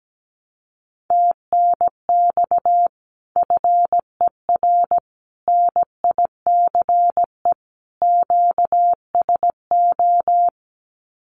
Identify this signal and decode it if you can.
Morse code, heard as a keyed tone, about 17 wpm, 700 Hz. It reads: TNX FER NICE QSO
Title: Common QSO Phrases Encore 17wpm